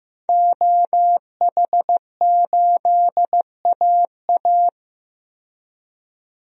Morse code, heard as a keyed tone, about 15 wpm, 700 Hz. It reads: OH8AA
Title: DXCC-100 Call Signs - 5 Characters - 15wpm